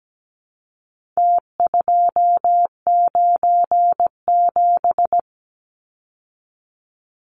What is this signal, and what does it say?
Morse code, heard as a keyed tone, about 17 wpm, 700 Hz. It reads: T297
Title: Numbers - Contest Style - 4-Digit 17wpm